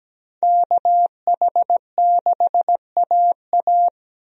Morse code, heard as a keyed tone, about 17 wpm, 700 Hz. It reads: KH6AA
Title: DXCC-100 Call Signs - 5 Characters - 17wpm